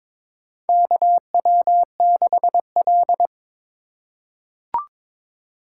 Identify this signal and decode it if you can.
Morse code, heard as a keyed tone, about 22 wpm, 700 Hz. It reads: KW6L
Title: USA Call Signs - 4 Characters - 4x Character Spacing - 22wpm